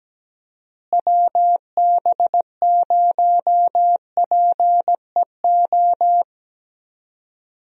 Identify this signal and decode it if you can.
Morse code, heard as a keyed tone, about 17 wpm, 700 Hz. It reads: WB0PEO
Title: USA Call Signs - 6 Characters - Repeated 2x - 17wpm